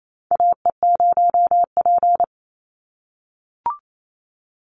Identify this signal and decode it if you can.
Morse code, heard as a keyed tone, about 28 wpm, 700 Hz. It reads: AE0P
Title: Call Signs - 28wpm